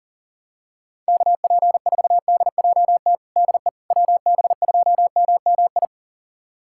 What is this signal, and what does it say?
Morse code, heard as a keyed tone, about 40 wpm, 700 Hz. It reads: KP4DJT DE WB2MMI
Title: USA Callsign DE Callsign - 6 Characters - 40wpm